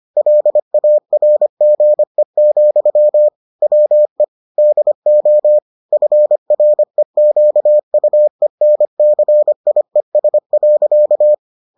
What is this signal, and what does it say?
Morse code, heard as a keyed tone, about 25 wpm, 600 Hz. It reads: LARGE, WE DO FREQUENCIES.